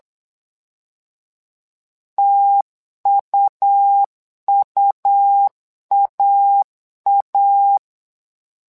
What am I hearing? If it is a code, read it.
Morse code, heard as a keyed tone, about 8 wpm, 800 Hz. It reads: TUUAA
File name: LEZIONI / 2 UHDN / 2 U 8wpm 2028